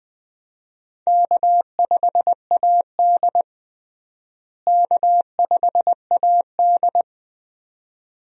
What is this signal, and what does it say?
Morse code, heard as a keyed tone, about 20 wpm, 700 Hz. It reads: K5AD K5AD
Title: USA Call Signs - 4 Characters - Repeated 2x - 20wpm